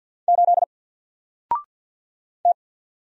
Morse code, heard as a keyed tone, about 49 wpm, 700 Hz. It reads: 8 T
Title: Single Character (Letter:Number:Punctuation:Prosign) Alt 50wpm